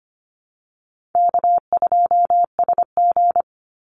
Morse code, heard as a keyed tone, about 25 wpm, 700 Hz. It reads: K2SG
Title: Easy Call Signs - 25wpm